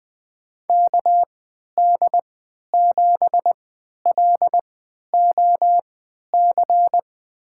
Morse code, heard as a keyed tone, about 20 wpm, spaced as for 11 wpm, 700 Hz. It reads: KD7LOC
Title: USA Call Signs - 6 Characters - 3x Character Spacing - 20wpm